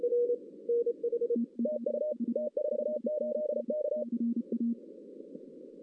Morse code, heard as a keyed tone, about 34 wpm, 600 Hz, with radio noise. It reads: TU T48K
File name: 2016 arrl dx cw / 2016-ARRL-DX-CW-K5TR- - 0-3